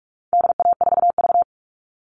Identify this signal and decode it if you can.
Morse code, heard as a keyed tone, about 44 wpm, 700 Hz. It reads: DA4V
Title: All Cut Numbers - 4-Digit 45wpm